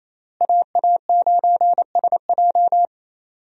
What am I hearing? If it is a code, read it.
Morse code, heard as a keyed tone, about 28 wpm, 700 Hz. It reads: AA9SJ
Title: USA Call Signs - 5 Characters - Repeated 3x - 28wpm